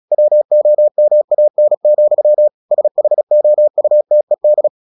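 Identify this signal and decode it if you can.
Morse code, heard as a keyed tone, about 36 wpm, 600 Hz. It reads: WOMAN, SHOUTED
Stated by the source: Morse practice at 36 wpm, text of Ken Follett's novel Never, from CW med Gnister.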